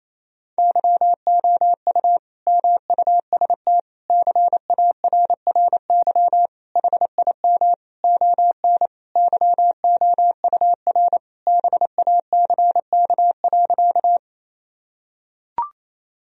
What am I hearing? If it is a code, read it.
Morse code, heard as a keyed tone, about 28 wpm, 700 Hz. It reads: YOU MUST CARRY HIM ON YOUR BACK.
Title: Sentences from Top 1000 Words - Review Part 1 - 28wpm